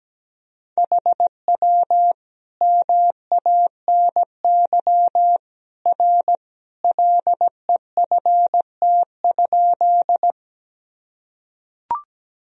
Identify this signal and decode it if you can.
Morse code, heard as a keyed tone, about 17 wpm, 700 Hz. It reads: HW MANY R LEFT?
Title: Sentences from Top 300 Words and Abbreviations 17wpm